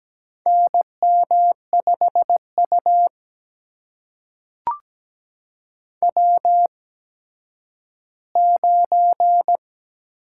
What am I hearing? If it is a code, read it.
Morse code, heard as a keyed tone, about 17 wpm, spaced as for 10 wpm, 700 Hz. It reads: NM5U W9
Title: USA Call Signs - 4 Characters - 8x Character Spacing - 17wpm